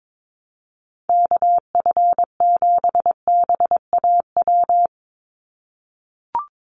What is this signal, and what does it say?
Morse code, heard as a keyed tone, about 22 wpm, 700 Hz. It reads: KF7BAW
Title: USA Call Signs 22wpm